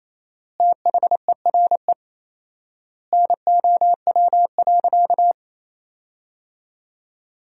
Morse code, heard as a keyed tone, about 27 wpm, 700 Hz. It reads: THERE NOW.
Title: 4x Word-Spacing - Sentences from Top 200 Words - 28wpm